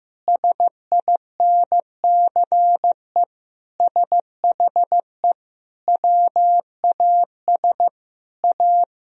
Morse code, heard as a keyed tone, about 15 wpm, 700 Hz. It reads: SINCE SHE WAS A
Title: Sentences from Top 1000 Words - Review Part 1 - 15wpm